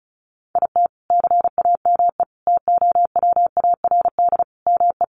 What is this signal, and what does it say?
Morse code, heard as a keyed tone, about 34 wpm, 700 Hz. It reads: IT CAME TOWARD ME
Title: Sentences from Top 1500 Words 35wpm